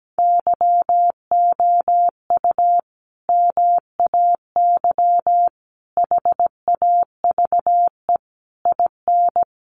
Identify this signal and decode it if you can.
Morse code, heard as a keyed tone, about 17 wpm, 700 Hz. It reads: YOU MAY HAVE IN
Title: Sentences from Top 1400 Words 17wpm